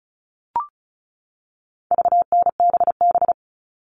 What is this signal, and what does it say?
Morse code, heard as a keyed tone, about 34 wpm, 700 Hz. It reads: VNBB
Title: All Cut Numbers - 4-Digit 35wpm